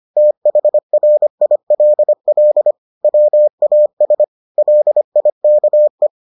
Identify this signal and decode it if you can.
Morse code, heard as a keyed tone, about 25 wpm, 600 Hz. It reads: THRILL WAS LIKE